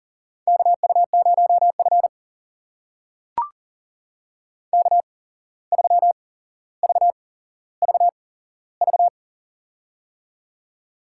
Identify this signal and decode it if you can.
Morse code, heard as a keyed tone, about 40 wpm, spaced as for 16 wpm, 700 Hz. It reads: KU0F K3VVV
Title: USA Call Signs - 8x Character Spacing - 40wpm